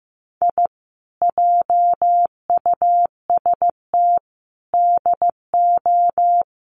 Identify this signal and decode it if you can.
Morse code, heard as a keyed tone, about 15 wpm, 700 Hz. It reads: I JUST DO
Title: Sets of 4 Words 15wpm